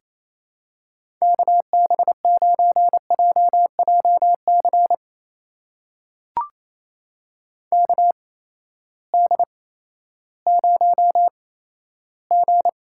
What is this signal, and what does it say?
Morse code, heard as a keyed tone, about 28 wpm, spaced as for 14 wpm, 700 Hz. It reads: KB9JJC KD0G
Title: USA Call Signs - 8x Character Spacing - 28wpm